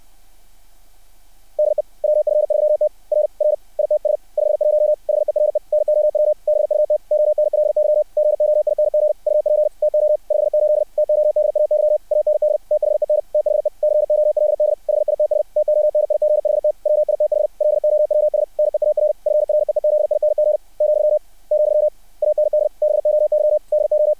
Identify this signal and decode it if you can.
Morse code, heard as a keyed tone, about 60 wpm, 600 Hz. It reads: DE WF3T R R ITS 51 HERE NOW BUT ONLY GOING UP TO 58 TODAY AND THEN THE COLD HITS TONIGHT WITH LOWS NEAR FREEZING , , AND BOY DO
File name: archived / CRSnet